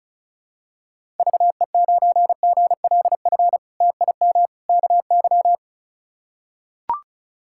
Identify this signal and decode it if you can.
Morse code, heard as a keyed tone, about 36 wpm, 700 Hz. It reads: VE9GLF TIM KY
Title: K1USN SST QSO Practice - 35wpm